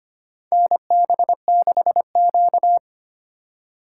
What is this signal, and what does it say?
Morse code, heard as a keyed tone, about 25 wpm, 700 Hz. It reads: NB6Q
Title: USA Call Signs - 4 Characters - 3x Character Spacing - 25wpm